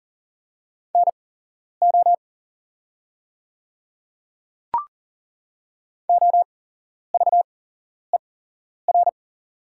Morse code, heard as a keyed tone, about 40 wpm, spaced as for 11 wpm, 690 Hz. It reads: NO OVER
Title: 8x Character Spacing - Top 100 Words - 40wpm